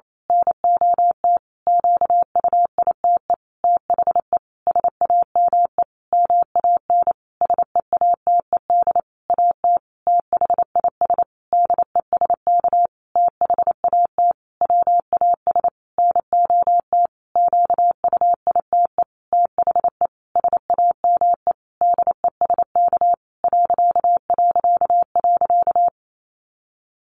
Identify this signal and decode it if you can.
Morse code, heard as a keyed tone, about 28 wpm, 700 Hz. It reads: NOT QUITE THE SAME MAN SEATED AT THIS DESK THAT WAS NOT QUITE THE SAME DESK ...
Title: A Sound Of Thunder 28wpm